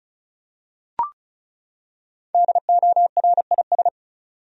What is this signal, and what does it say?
Morse code, heard as a keyed tone, about 36 wpm, 700 Hz. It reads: DORIS
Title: USA Common Names - Max 5 Letters - 35wpm